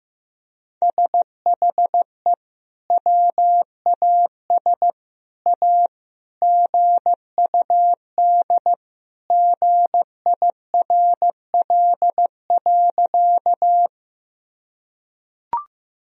Morse code, heard as a keyed tone, about 15 wpm, 700 Hz. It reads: SHE WAS A GUD GIRL.
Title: Sentences from Top 400 Words and Abbreviations 15wpm